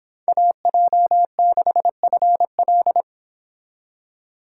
Morse code, heard as a keyed tone, about 26 wpm, 700 Hz. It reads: AJ6FL